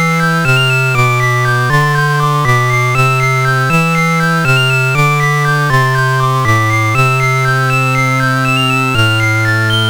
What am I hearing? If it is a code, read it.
There is no code.